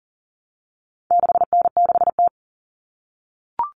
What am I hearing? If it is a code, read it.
Morse code, heard as a keyed tone, about 40 wpm, 700 Hz. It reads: 6N6T
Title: All Cut Numbers - 4-Digit 40wpm